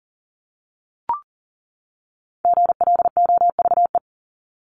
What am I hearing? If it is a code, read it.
Morse code, heard as a keyed tone, about 40 wpm, 700 Hz. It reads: GLOVE
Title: Top 1751-2000 Words 40wpm